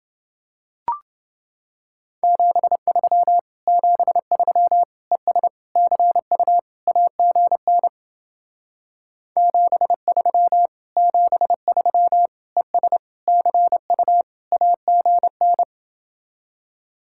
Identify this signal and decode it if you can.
Morse code, heard as a keyed tone, about 28 wpm, 700 Hz. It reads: 73 73 ES CU AGN 73 73 ES CU AGN